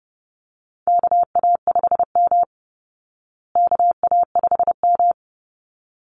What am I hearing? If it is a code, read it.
Morse code, heard as a keyed tone, about 30 wpm, 700 Hz. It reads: KA5M KA5M
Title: Call Signs Repeated 3x - 30wpm